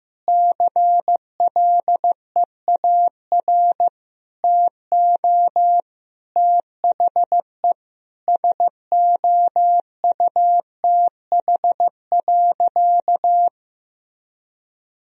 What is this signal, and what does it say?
Morse code, heard as a keyed tone, about 15 wpm, 700 Hz. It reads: CLEAR TO THE SOUTH.
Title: Sentences from Top 1000 Words - Review Part 2 - 15wpm